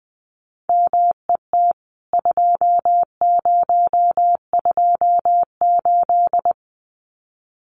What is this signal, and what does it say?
Morse code, heard as a keyed tone, about 20 wpm, 700 Hz. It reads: MET 2028
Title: CWT Exchange 20wpm